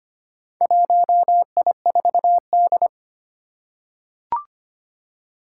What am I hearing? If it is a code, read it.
Morse code, heard as a keyed tone, about 25 wpm, 700 Hz. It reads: 1I4D
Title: Four Challenging Letter-Number Combinations 25wpm